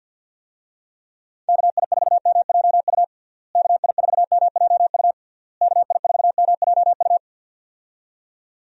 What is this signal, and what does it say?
Morse code, heard as a keyed tone, about 52 wpm, 700 Hz. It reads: KI4MJU KI4MJU KI4MJU
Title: USA Call Signs - 6 Characters - Repeated 3x - 50wpm